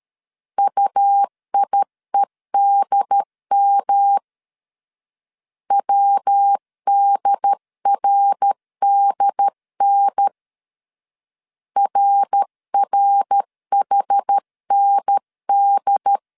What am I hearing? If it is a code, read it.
Morse code, heard as a keyed tone, about 13 wpm, 800 Hz. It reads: UIEDM WDRDN RRHND
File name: CW / DET / 1 3 1500C S A 12wpm 2905